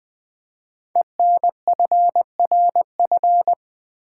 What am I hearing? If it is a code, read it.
Morse code, heard as a keyed tone, about 20 wpm, 700 Hz. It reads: ENFRF
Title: Code Groups - 5 Letters - 20wpm